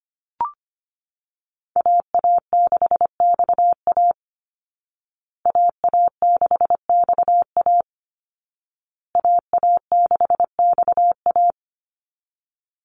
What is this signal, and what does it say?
Morse code, heard as a keyed tone, about 25 wpm, 700 Hz. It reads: AA6XA AA6XA AA6XA
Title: Call Signs Repeated 2x - 25wpm